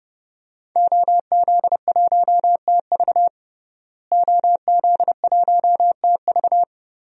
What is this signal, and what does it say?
Morse code, heard as a keyed tone, about 30 wpm, 700 Hz. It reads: OZ1TV OZ1TV
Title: DXCC-100 Call Signs - 5 Characters - Repeated 3x - 30wpm